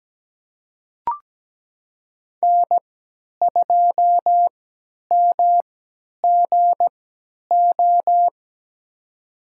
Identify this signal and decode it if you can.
Morse code, heard as a keyed tone, about 17 wpm, spaced as for 10 wpm, 700 Hz. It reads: N2MGO
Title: USA Call Signs - 5 Characters - 3x Character Spacing - 17wpm